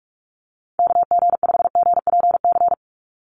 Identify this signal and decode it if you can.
Morse code, heard as a keyed tone, about 46 wpm, 700 Hz. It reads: KG5GPC
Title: USA Call Signs 45wpm